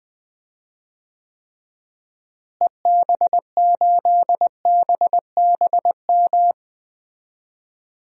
Morse code, heard as a keyed tone, about 20 wpm, 700 Hz. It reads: EB8BBM
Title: DXCC-100 Call Signs - 6 Characters - Repeated 2x - 20wpm